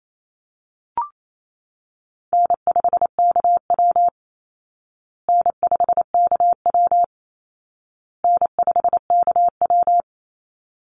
Morse code, heard as a keyed tone, about 28 wpm, 700 Hz. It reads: N5KW N5KW N5KW
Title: Easy Call Signs Repeated 3x - 28wpm